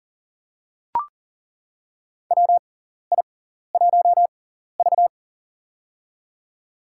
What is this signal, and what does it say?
Morse code, heard as a keyed tone, about 40 wpm, spaced as for 14 wpm, 700 Hz. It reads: WI1V
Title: USA Call Signs - 4 Characters - 6x Character Spacing - 40wpm